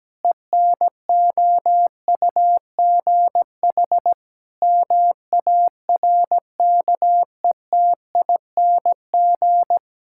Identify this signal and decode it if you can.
Morse code, heard as a keyed tone, about 17 wpm, 700 Hz. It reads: ENOUGH MARKETING
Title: Sentences from Top 1000 Words - Review Part 5 - 17wpm